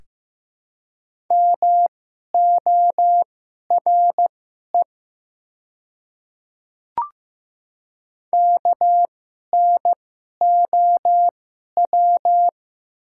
Morse code, heard as a keyed tone, about 15 wpm, spaced as for 11 wpm, 700 Hz. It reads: MORE KNOW
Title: 2x Character Spacing - Top 200 Words - 15wpm